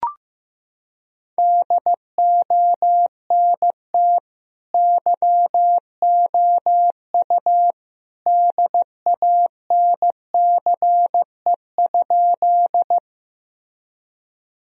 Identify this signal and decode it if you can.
Morse code, heard as a keyed tone, about 16 wpm, 700 Hz. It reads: DONT YOU DANCE?
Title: Sentences from Top 1000 Words - Review Part 4 - 15wpm